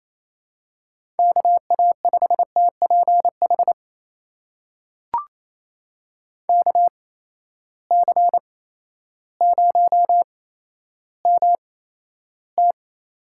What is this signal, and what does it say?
Morse code, heard as a keyed tone, about 28 wpm, spaced as for 13 wpm, 700 Hz. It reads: KA5TPH KC0MT
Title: USA Call Signs - 6 Characters - 8x Character Spacing - 28wpm